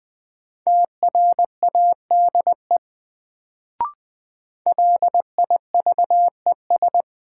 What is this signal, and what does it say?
Morse code, heard as a keyed tone, about 20 wpm, 700 Hz. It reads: TRADE LIVES
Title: Sets of 5 Words Encore - New Word Drill - 20wpm